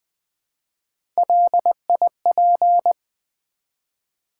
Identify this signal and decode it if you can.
Morse code, heard as a keyed tone, about 20 wpm, 700 Hz. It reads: LIP
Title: Build A Word - 1 Letter At A Time - 3-Letter Words - 20wpm